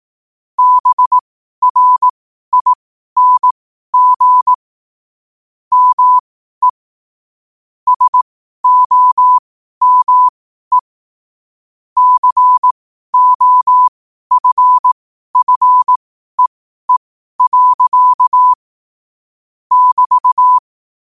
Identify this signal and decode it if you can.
Morse code, heard as a keyed tone, about 18 wpm, spaced as for 12 wpm, 1000 Hz. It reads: BRING ME SOME COFFEE. <BT>
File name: qotdfiles / 12wpm